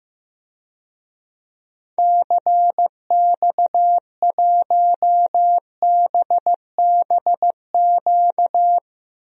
Morse code, heard as a keyed tone, about 15 wpm, 700 Hz. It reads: CX1BBQ
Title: DXCC-100 Call Signs - 6 Characters - 15wpm